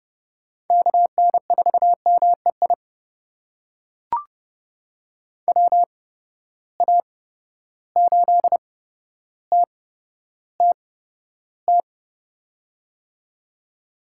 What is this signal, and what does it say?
Morse code, heard as a keyed tone, about 30 wpm, spaced as for 13 wpm, 700 Hz. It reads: KN4MEI WA8TTT
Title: USA Call Signs - 6 Characters - 8x Character Spacing - 30wpm